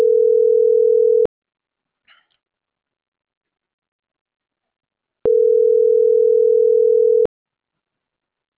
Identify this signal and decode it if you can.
Ringback tone (North America), 440 + 480 Hz.